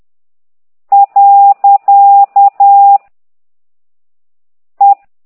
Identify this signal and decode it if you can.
Morse code, heard as a keyed tone, about 10 wpm, 800 Hz. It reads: . E